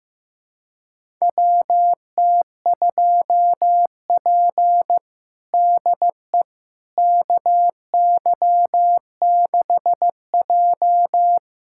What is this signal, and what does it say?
Morse code, heard as a keyed tone, about 15 wpm, 700 Hz. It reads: WT2P DE KY6J
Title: USA Callsign DE Callsign - 4 Characters - 15wpm